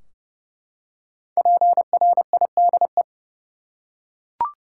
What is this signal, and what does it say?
Morse code, heard as a keyed tone, about 31 wpm, 700 Hz. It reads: PRIDE
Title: Five Letter Words - 30wpm